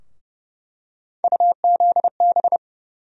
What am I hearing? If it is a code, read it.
Morse code, heard as a keyed tone, about 30 wpm, 700 Hz. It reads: UZB
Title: USA Call Sign Postfixes - 3 Letters 30wpm